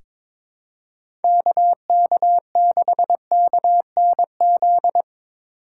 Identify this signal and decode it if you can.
Morse code, heard as a keyed tone, about 22 wpm, 700 Hz. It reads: KK6KNZ
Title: USA Call Signs - 6 Characters - 4x Character Spacing - 22wpm